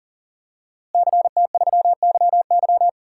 Morse code, heard as a keyed tone, about 38 wpm, 700 Hz. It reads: CT3YY
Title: DXCC-100 Call Signs - 5 Characters - Repeated 3x - 40wpm